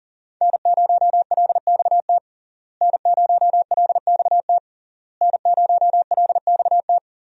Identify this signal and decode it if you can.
Morse code, heard as a keyed tone, about 40 wpm, 700 Hz. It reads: N0LXT N0LXT N0LXT
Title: USA Call Signs - 5 Characters - Repeated 3x - 40wpm